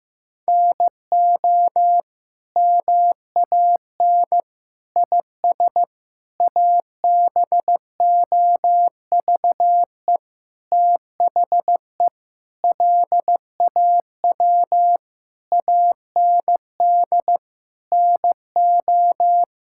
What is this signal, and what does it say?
Morse code, heard as a keyed tone, about 15 wpm, 700 Hz. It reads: NO MAN IS ABOVE THE LAW AND NO
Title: Sentences from Top 1000 Words - Review Part 5 - 15wpm